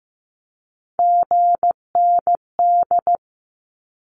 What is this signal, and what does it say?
Morse code, heard as a keyed tone, about 15 wpm, 700 Hz. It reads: GND
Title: Abbreviations Repeated 3x - 15wpm